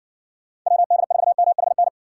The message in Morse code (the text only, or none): WD4MHN